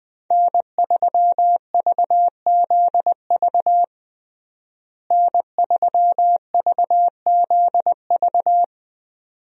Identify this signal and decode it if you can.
Morse code, heard as a keyed tone, about 20 wpm, 700 Hz. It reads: N3VZV N3VZV
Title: USA Call Signs - 5 Characters - Repeated 2x - 20wpm